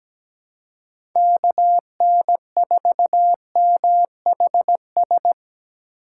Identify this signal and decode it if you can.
Morse code, heard as a keyed tone, about 17 wpm, 700 Hz. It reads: KN4MHS
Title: USA Call Signs - 2x Character Spacing - 17wpm